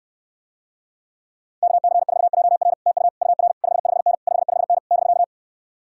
Morse code, heard as a keyed tone, about 66 wpm, 700 Hz. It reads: KC4JR TU UR 55N 55N <BK>